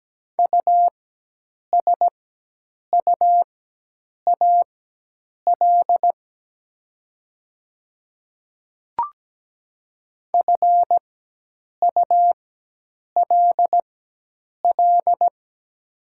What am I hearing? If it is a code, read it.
Morse code, heard as a keyed tone, about 17 wpm, spaced as for 8 wpm, 700 Hz. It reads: USUAL FULL
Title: 4x Character Spacing - Top 500 Words - 17wpm